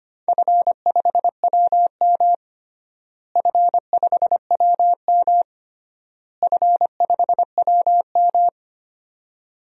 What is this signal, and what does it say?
Morse code, heard as a keyed tone, about 25 wpm, 700 Hz. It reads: F5WM F5WM F5WM
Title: DXCC-100 Call Signs - 4 Characters - Repeated 3x - 25wpm